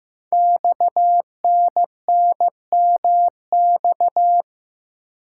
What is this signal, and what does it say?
Morse code, heard as a keyed tone, about 15 wpm, 700 Hz. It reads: XNNMX
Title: Code Groups - 5 Letters - 15wpm